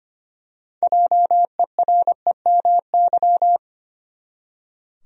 Morse code, heard as a keyed tone, about 25 wpm, 700 Hz. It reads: JEREMY
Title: USA Common Names - 25wpm